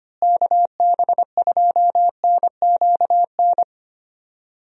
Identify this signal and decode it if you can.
Morse code, heard as a keyed tone, about 25 wpm, 700 Hz. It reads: KB2NQN
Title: USA Call Signs - 6 Characters - Repeated 2x - 25wpm